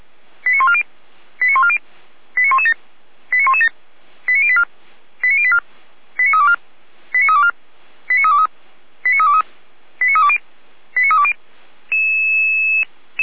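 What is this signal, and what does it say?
Five-tone selective call (ZVEI), heard as quick runs of five tones, tones of about 70 ms. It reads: 89140 89140 89117 89117 89084 89084 89324 89324 89323 89323 89320 89320
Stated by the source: Five-tone ZVEI selective call received by SDR, from audio.f85360.feuerwehr-orlamuende.org (Feuerwehr Orlamünde).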